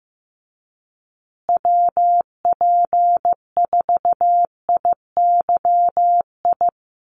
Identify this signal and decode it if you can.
Morse code, heard as a keyed tone, about 15 wpm, 700 Hz. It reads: WP4IYI
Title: USA Call Signs Repeated 3x 15wpm